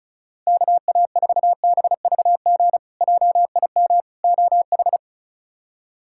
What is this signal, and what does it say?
Morse code, heard as a keyed tone, about 34 wpm, 700 Hz. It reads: KA4BVG JIM OH
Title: K1USN SST QSO Practice - 35wpm